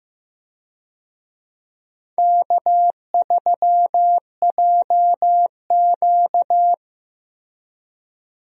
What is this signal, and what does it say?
Morse code, heard as a keyed tone, about 16 wpm, 700 Hz. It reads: K3JQ
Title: USA Call Signs - 4 Characters - Repeated 3x - 15wpm